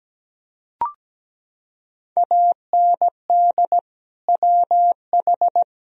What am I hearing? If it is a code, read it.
Morse code, heard as a keyed tone, about 17 wpm, 700 Hz. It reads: AND WH
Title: Sentences from Top 1000 Words - Review Part 3 - 17wpm